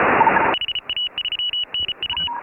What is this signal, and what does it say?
Morse code, heard as a keyed tone, about 34 wpm, 2950 Hz. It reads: HA3NU
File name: AUDIO2018 / HA3FMR CW